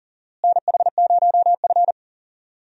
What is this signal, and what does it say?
Morse code, heard as a keyed tone, about 40 wpm, 700 Hz. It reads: NH0F
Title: USA Call Signs - 4 Characters - Repeated 2x - 40wpm